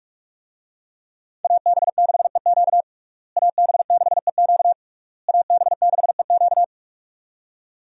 Morse code, heard as a keyed tone, about 46 wpm, 700 Hz. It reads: AB6EQ AB6EQ AB6EQ
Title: USA Call Signs - 5 Characters - Repeated 3x - 45wpm